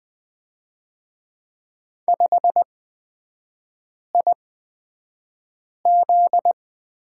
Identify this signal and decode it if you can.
Morse code, heard as a keyed tone, about 20 wpm, spaced as for 5 wpm, 700 Hz. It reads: 5IZ
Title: 243 Lesson 39e - Letter z - Three Characters 20wpm